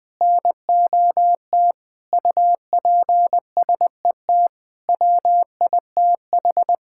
Morse code, heard as a keyed tone, about 20 wpm, 700 Hz. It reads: NOT UPSET WITH
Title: Sentences from Top 1000 Words - Review Part 5 - 20wpm